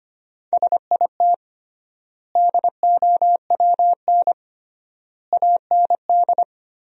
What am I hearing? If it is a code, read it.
Morse code, heard as a keyed tone, about 25 wpm, 700 Hz. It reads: SIT DOWN AND
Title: 3x Word-Spacing - Sentences from Top 500 Words - 25wpm